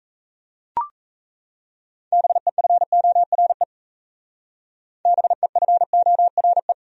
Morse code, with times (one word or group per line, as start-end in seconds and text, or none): 2.1-3.6 BEFORE
5.1-6.7 BEFORE